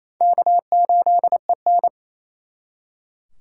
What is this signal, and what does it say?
Morse code, heard as a keyed tone, about 34 wpm, 700 Hz. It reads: K8EN
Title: Speed Racing - USA 1x2 Call Signs 28wpm